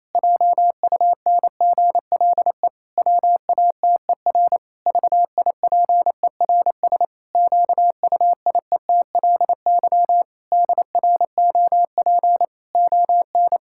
Morse code, heard as a keyed tone, about 27 wpm, 700 Hz. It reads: JUNGLE WATER VIPERS QUIETLY DROP ON
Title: Pangram Challenge v2 - 28wpm